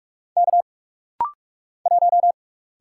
Morse code, heard as a keyed tone, about 45 wpm, 700 Hz. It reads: K 1
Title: Single Letter-Number Alt - Rapid-Fire 45wpm